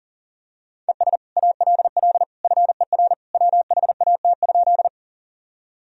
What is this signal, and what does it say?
Morse code, heard as a keyed tone, about 40 wpm, 700 Hz. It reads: ES ALL FER WHAT?
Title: Sentences from Top 100 Words and Abbreviations 40wpm